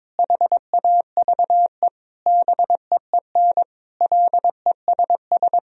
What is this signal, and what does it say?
Morse code, heard as a keyed tone, about 22 wpm, 700 Hz. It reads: HAVE BEEN LESS
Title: Sentences from Top 1000 Words - Review Part 2 - 22wpm